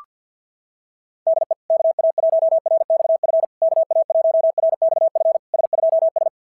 Morse code, heard as a keyed tone, about 50 wpm, 660 Hz. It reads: DE KA1RXF KA1RXF S2S
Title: SOTA QSO Practice 50wpm